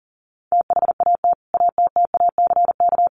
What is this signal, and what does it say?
Morse code, heard as a keyed tone, about 40 wpm, 700 Hz. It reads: THAT ATTACK
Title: Sentences from Top 1500 Words 40wpm